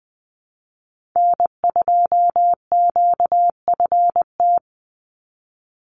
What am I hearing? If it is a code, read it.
Morse code, heard as a keyed tone, about 20 wpm, 700 Hz. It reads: N2QFT
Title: USA Call Signs 20wpm